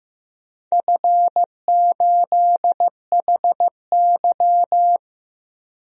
Morse code, heard as a keyed tone, about 15 wpm, 700 Hz. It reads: F8HY
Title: DXCC-100 Call Signs - 4 Characters - Repeated 3x - 15wpm